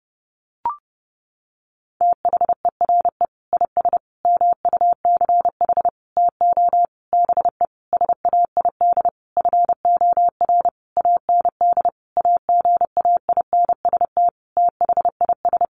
Morse code, heard as a keyed tone, about 30 wpm, 700 Hz. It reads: THERE IS MUCH TO BE SAID FOR AND AGAINST THIS
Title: Sentences from Top 1300 Words 30wpm